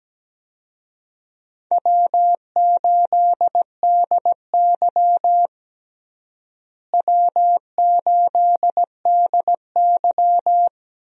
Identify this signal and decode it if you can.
Morse code, heard as a keyed tone, about 17 wpm, 700 Hz. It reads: W8DY W8DY
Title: USA Call Signs - 4 Characters - Repeated 2x - 17wpm